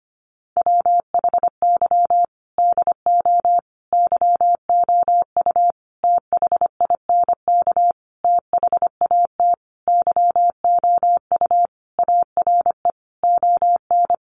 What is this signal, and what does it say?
Morse code, heard as a keyed tone, about 25 wpm, 700 Hz. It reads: WHY DO YOU THINK THAT YOU ARE ON
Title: Sentences from Top 1400 Words 25wpm